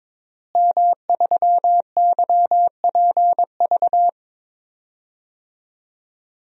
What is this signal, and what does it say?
Morse code, heard as a keyed tone, about 22 wpm, 700 Hz. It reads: M3YPV
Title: DXCC-100 Call Signs - 5 Characters - Repeated 2x - 22wpm